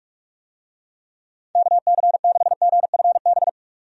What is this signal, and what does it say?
Morse code, heard as a keyed tone, about 46 wpm, 700 Hz. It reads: KC6GFB
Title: USA Call Signs - 6 Characters - 45wpm